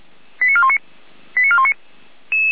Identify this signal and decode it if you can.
Five-tone selective call (ZVEI), heard as quick runs of five tones, tones of about 70 ms. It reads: 89419 89419